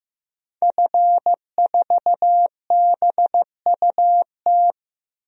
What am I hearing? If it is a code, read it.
Morse code, heard as a keyed tone, about 15 wpm, 700 Hz. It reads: F4BUT
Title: DXCC-100 Call Signs - 5 Characters - Repeated 3x - 15wpm